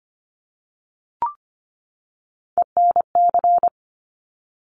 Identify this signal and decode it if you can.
Morse code, heard as a keyed tone, about 25 wpm, 700 Hz. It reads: ENC
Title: Top Three Letter Combinations in Words 25wpm